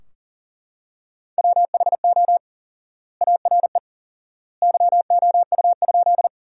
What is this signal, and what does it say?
Morse code, heard as a keyed tone, about 37 wpm, 700 Hz. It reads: WHO ARE YOU?
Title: 4x Word-Spacing - Sentences from Top 100 Words - 40wpm